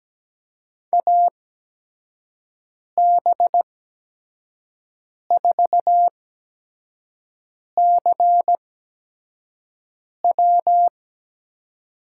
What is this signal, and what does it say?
Morse code, heard as a keyed tone, about 17 wpm, spaced as for 5 wpm, 700 Hz. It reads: AB4CW
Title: USA Call Signs - 5 Characters - 8x Character Spacing - 17wpm